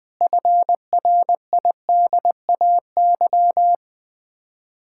Morse code, heard as a keyed tone, about 20 wpm, 700 Hz. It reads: FRIDAY
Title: Build A Phrase - 5-Word Phrases - 20wpm